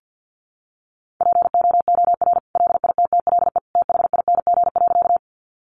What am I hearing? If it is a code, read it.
Morse code, heard as a keyed tone, about 50 wpm, 700 Hz. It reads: POOR LITTLE THING.
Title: Sentences from Top 1000 Words 50wpm